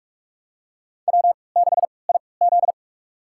Morse code, heard as a keyed tone, about 47 wpm, spaced as for 26 wpm, 700 Hz. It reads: W6IZ
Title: USA Call Signs - 3x Character Spacing - 45wpm